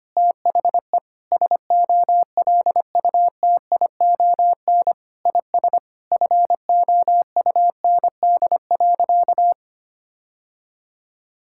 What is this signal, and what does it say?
Morse code, heard as a keyed tone, about 25 wpm, 700 Hz. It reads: THE SOLUTION IS FOUND.
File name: practice / 25wpm / Sentences with 9-Word Phrases - 25wpm